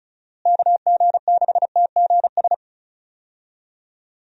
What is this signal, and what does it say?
Morse code, heard as a keyed tone, about 34 wpm, 700 Hz. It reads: KG6TGS
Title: USA Call Signs - 6 Characters - Repeated 2x - 35wpm